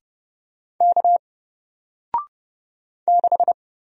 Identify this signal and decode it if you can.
Morse code, heard as a keyed tone, about 30 wpm, 700 Hz. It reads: K 6
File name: Single-Letter-Number-30wpm